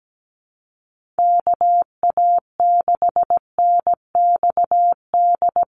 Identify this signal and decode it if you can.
Morse code, heard as a keyed tone, about 17 wpm, 700 Hz. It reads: KA6NXD
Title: USA Call Signs 17wpm